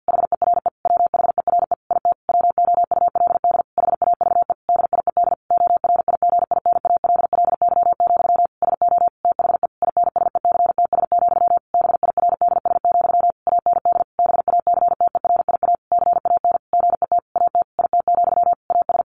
Morse code, heard as a keyed tone, about 50 wpm, 700 Hz. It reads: HERE WHERE IT WOULD HAVE DIED ORIGINALLY, SO THE INSECTS, BIRDS, AND BACTERIA CAN GET AT IT, AS
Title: A Sound Of Thunder 50wpm